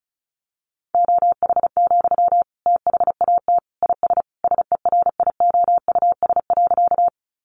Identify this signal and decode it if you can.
Morse code, heard as a keyed tone, about 34 wpm, 700 Hz. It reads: OH, THAT IS SERIOUS.
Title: Sentences from Top 1100 Words 35wpm